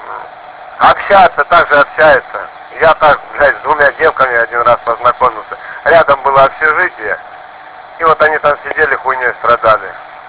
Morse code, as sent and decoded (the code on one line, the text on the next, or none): none